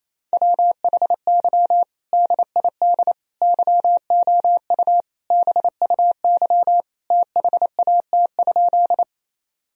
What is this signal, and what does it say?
Morse code, heard as a keyed tone, about 28 wpm, 700 Hz. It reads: WHY DID YOU BUY THAT?
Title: Sentences from Top 1000 Words - Review Part 4 - 28wpm